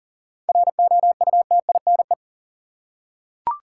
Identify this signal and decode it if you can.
Morse code, heard as a keyed tone, about 40 wpm, 700 Hz. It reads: ROUTINE
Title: Seven Letter Words - 40wpm